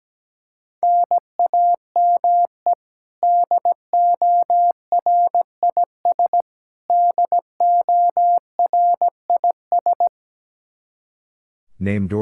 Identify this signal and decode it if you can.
Morse code, heard as a keyed tone, about 17 wpm, 700 Hz. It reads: NAME DORIS DORIS
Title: Practice QSO - Exchange 1 - Beginning - 17wpm